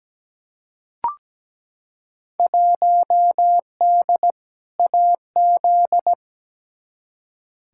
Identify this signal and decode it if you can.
Morse code, heard as a keyed tone, about 17 wpm, 700 Hz. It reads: 1D AZ
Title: Field Day Exchanges - 17wpm